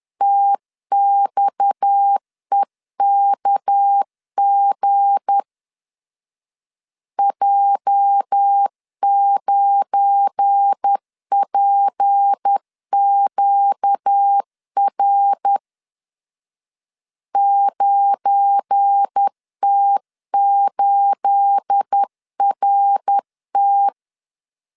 Morse code, heard as a keyed tone, about 10 wpm, 800 Hz. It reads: TXEKG J9PQR 9T8RT
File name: LEZIONI / 9 28 10wpm 2842